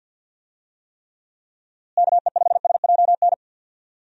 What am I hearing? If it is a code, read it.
Morse code, heard as a keyed tone, about 46 wpm, 700 Hz. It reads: KE5SJN
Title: USA Call Signs - 6 Characters - Repeated 3x - 50wpm